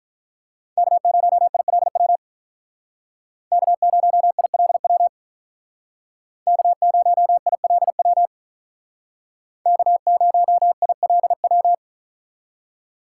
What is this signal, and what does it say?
Morse code, heard as a keyed tone, about 46 wpm, 700 Hz. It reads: K0ILW K0ILW K0ILW K0ILW